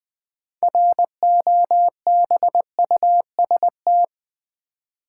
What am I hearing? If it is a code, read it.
Morse code, heard as a keyed tone, about 20 wpm, 700 Hz. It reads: ROBUST
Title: 6 Letter Words - First 3 Letters Given - 20wpm